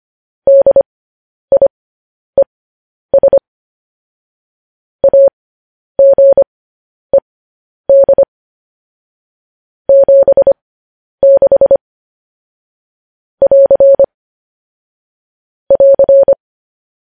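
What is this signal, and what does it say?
Morse code, heard as a keyed tone, about 25 wpm, spaced as for 10 wpm, 550 Hz. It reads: DIES AGED 76 <AR> <AR>
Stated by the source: Morse code news headlines sent as a reading transcription, preceded by vvv.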